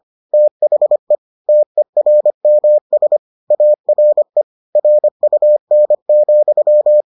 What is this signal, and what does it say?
Morse code, heard as a keyed tone, about 25 wpm, 600 Hz. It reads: THE TERMS ARE RUN,